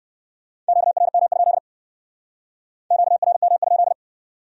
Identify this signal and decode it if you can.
Morse code, heard as a keyed tone, about 57 wpm, 700 Hz. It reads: QRM? QRM?